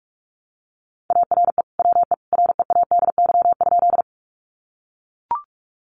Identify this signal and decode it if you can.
Morse code, heard as a keyed tone, about 44 wpm, 700 Hz. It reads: ARE WE READY?
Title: Sentences from Top 400 Words 45wpm